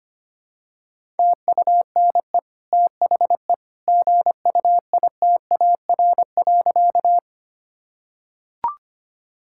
Morse code, heard as a keyed tone, about 25 wpm, 700 Hz. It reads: TUNE THE GUITAR.
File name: Easy Sentences with Hints - 25wpm